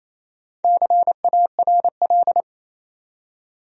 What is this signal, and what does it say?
Morse code, heard as a keyed tone, about 27 wpm, 700 Hz. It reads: CARL
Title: USA Common Names - Max 4 Letters - 28wpm